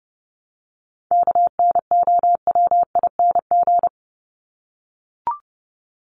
Morse code, heard as a keyed tone, about 30 wpm, 700 Hz. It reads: KNOWING
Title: ING Suffix - Top 100 Words with Base Words 30wpm